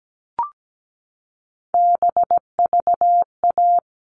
Morse code, heard as a keyed tone, about 17 wpm, 700 Hz. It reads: BVA
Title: All Cut Numbers - 3-Digit 17wpm